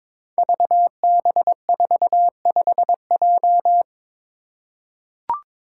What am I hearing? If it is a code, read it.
Morse code, heard as a keyed tone, about 22 wpm, 700 Hz. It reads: VB45J
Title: Code Groups - 5 Letter:Numbers - 22wpm